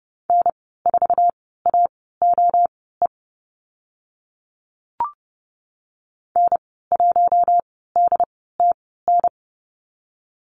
Morse code, heard as a keyed tone, about 30 wpm, spaced as for 17 wpm, 700 Hz. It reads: N4AOE N1DTN
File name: Lesson 14g - Number 2 - Call Signs 30wpm